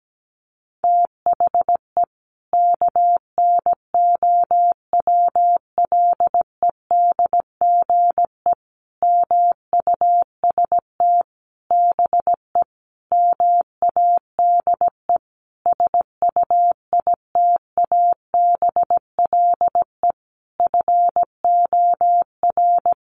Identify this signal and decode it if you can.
Morse code, heard as a keyed tone, about 17 wpm, 700 Hz. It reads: THE KNOWLEDGE MUST BE MADE SUITABLE FOR
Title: Sentences from Top 1400 Words 17wpm